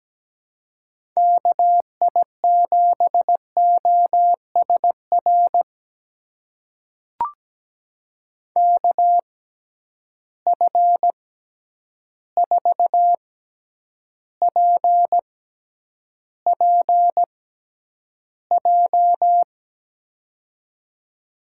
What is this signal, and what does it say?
Morse code, heard as a keyed tone, about 17 wpm, spaced as for 9 wpm, 700 Hz. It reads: KI7OSR KF4PPJ
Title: USA Call Signs - 6 Characters - 6x Character Spacing - 17wpm